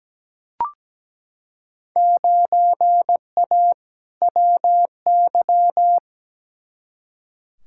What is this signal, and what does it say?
Morse code, heard as a keyed tone, about 17 wpm, 700 Hz. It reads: 9A WY
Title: Field Day Exchanges Encore - 17wpm